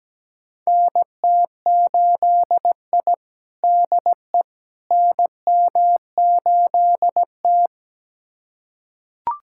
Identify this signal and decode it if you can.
Morse code, heard as a keyed tone, about 17 wpm, 700 Hz. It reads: NT8I DE NM8T
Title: USA Callsign DE Callsign - 4 Characters - 17wpm